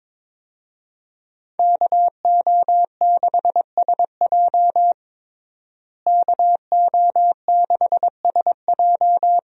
Morse code, heard as a keyed tone, about 22 wpm, 700 Hz. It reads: KO6SJ KO6SJ
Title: USA Call Signs - 5 Characters - Repeated 2x - 22wpm